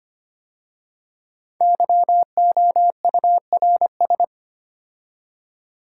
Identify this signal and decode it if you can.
Morse code, heard as a keyed tone, about 25 wpm, 700 Hz. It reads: YOURS
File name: 5 Letter Words - First 3 Letters Given - 25wpm